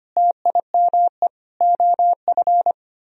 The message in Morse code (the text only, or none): TIME OF